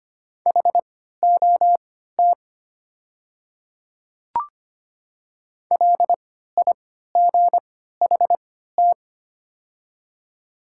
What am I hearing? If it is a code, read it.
Morse code, heard as a keyed tone, about 25 wpm, spaced as for 14 wpm, 690 Hz. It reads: HOT LIGHT
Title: 3x Character Spacing - Top 200 Words - 25wpm